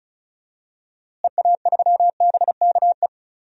Morse code, heard as a keyed tone, about 36 wpm, 700 Hz. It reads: EA3BKE
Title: DXCC-100 Call Signs - 6 Characters - Repeated 3x - 35wpm